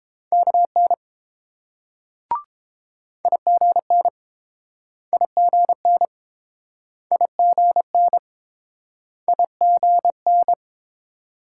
Morse code, heard as a keyed tone, about 30 wpm, 700 Hz. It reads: KN IGN IGN IGN IGN